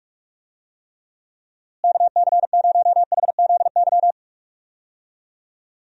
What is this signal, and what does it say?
Morse code, heard as a keyed tone, about 46 wpm, 700 Hz. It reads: KC0HZY
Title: USA Call Signs - 6 Characters - Repeated 2x - 45wpm